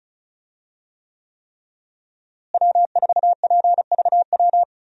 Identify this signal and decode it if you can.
Morse code, heard as a keyed tone, about 36 wpm, 700 Hz. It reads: W4PVW
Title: USA Call Signs - 5 Characters - Repeated 3x - 35wpm